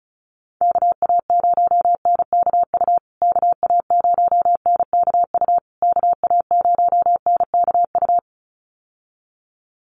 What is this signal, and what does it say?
Morse code, heard as a keyed tone, about 34 wpm, 700 Hz. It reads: KA0NKU KA0NKU KA0NKU
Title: USA Call Signs Repeated 3x 35wpm